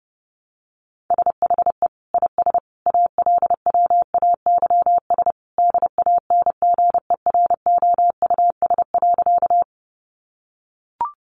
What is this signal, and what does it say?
Morse code, heard as a keyed tone, about 30 wpm, 700 Hz. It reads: SHE IS ALWAYS DANGEROUS.
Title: Sentences from Top 1200 Words 30wpm